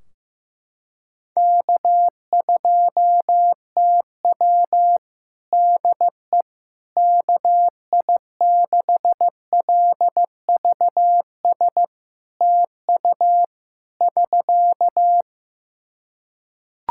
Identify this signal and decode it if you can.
Morse code, heard as a keyed tone, about 15 wpm, 700 Hz. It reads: K2TW DE KI6LVS TU <SK>
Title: Practice QSO - Exchange 4 - Ending - 15wpm